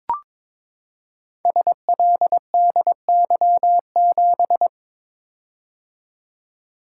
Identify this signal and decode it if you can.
Morse code, heard as a keyed tone, about 22 wpm, 700 Hz. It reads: SLDY7
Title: Code Groups - 5 Letter:Numbers - 22wpm